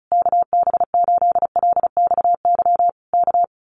Morse code, heard as a keyed tone, about 34 wpm, 700 Hz. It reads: KB8LXY K
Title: Common QSO Phrases Encore 35wpm